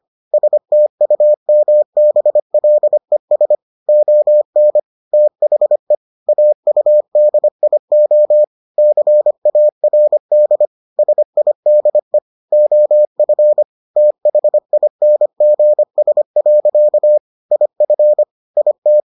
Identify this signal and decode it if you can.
Morse code, heard as a keyed tone, about 25 wpm, 600 Hz. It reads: STUMBLES ON THE AUDIO CARD SIDE OF THINGS. IF IT